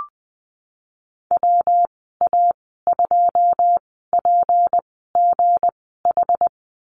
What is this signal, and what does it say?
Morse code, heard as a keyed tone, about 20 wpm, spaced as for 15 wpm, 700 Hz. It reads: WA2PGH
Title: 167 Lesson 27g - Number 7 - Call Signs 20wpm